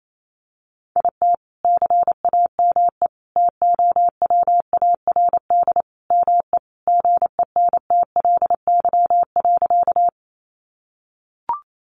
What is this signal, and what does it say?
Morse code, heard as a keyed tone, about 28 wpm, 700 Hz. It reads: IT CAME TOWARD ME GENTLY.
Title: Sentences from Top 1500 Words 28wpm